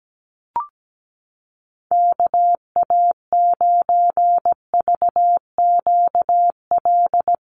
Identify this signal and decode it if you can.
Morse code, heard as a keyed tone, about 17 wpm, 700 Hz. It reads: KA9VQL
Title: USA Call Signs Repeated 3x 17wpm